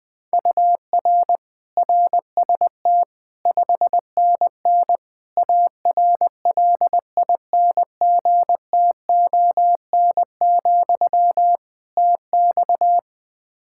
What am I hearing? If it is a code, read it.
Morse code, heard as a keyed tone, about 20 wpm, 700 Hz. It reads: UR RST 5NN ARLINGTON, TX
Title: Practice QSO - Exchange 1 - Beginning - 20wpm